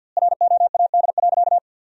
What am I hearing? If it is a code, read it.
Morse code, heard as a keyed tone, about 50 wpm, 700 Hz. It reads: ROAD.
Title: 2x Word-Spacing - Sentences from Top 400 Words - 50wpm